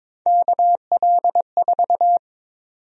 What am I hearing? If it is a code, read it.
Morse code, heard as a keyed tone, about 23 wpm, 700 Hz. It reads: KL4
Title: USA Call Sign Prefixes - 22wpm